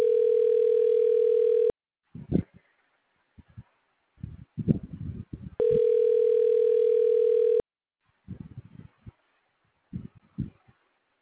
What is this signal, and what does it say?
Ringback tone (North America), 440 + 480 Hz.